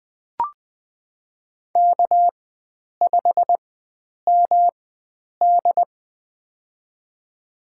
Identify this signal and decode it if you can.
Morse code, heard as a keyed tone, about 20 wpm, spaced as for 9 wpm, 700 Hz. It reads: K5MD
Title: USA Call Signs - 4 Characters - 4x Character Spacing - 20wpm